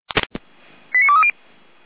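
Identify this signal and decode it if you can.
Five-tone selective call (ZVEI), heard as quick runs of five tones, tones of about 70 ms. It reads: 89230
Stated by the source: ZVEI five-tone selective call received by SDR.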